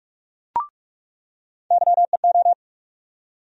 Morse code, heard as a keyed tone, about 42 wpm, 700 Hz. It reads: YEO
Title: USA Call Sign Postfixes - 3 Letters 45wpm